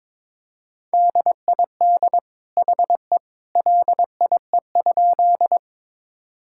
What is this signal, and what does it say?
Morse code, heard as a keyed tone, about 22 wpm, 700 Hz. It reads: DID HE LIE?
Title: Sentences from Top 1600 Words 22wpm